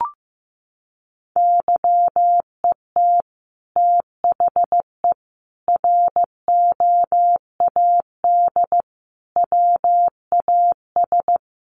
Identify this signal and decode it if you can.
Morse code, heard as a keyed tone, about 15 wpm, 700 Hz. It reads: YET THE ROAD WAS
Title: Sentences from Top 1400 Words 15wpm